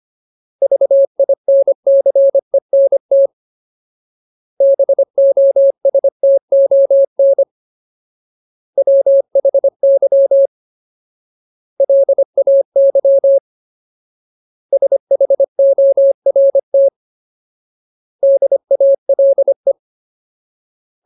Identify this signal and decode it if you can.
Morse code, heard as a keyed tone, about 25 wpm, 550 Hz. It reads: VINCENT BOSTON WHY LAY SHORT DALE
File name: Ultimate Word List / 25WPM Ultimate